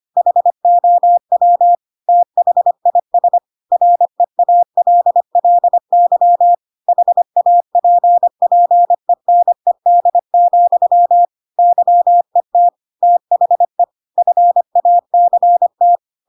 Morse code, heard as a keyed tone, about 25 wpm, 700 Hz. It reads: HOW THIS REALLY HAPPENED, YET THE FACT
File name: Pinocchio / Pinocchio ch1 25 25 ik1hge